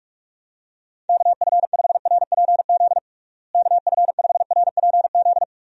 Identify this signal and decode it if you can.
Morse code, heard as a keyed tone, about 46 wpm, 700 Hz. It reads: KF5RPZ KF5RPZ
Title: USA Call Signs - 6 Characters - Repeated 2x - 45wpm